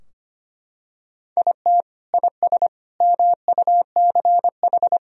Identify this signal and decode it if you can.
Morse code, heard as a keyed tone, about 25 wpm, 700 Hz. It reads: IT IS MUCH
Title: Sentences from Top 1000 Words - Review Part 3 - 25wpm